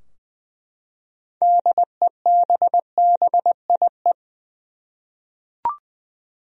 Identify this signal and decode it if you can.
Morse code, heard as a keyed tone, about 20 wpm, 700 Hz. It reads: DEBBIE
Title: USA Common Names - 20wpm